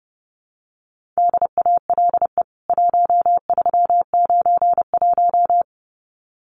Morse code, heard as a keyed tone, about 30 wpm, 700 Hz. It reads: DALE 1391
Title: CWT Exchange 30wpm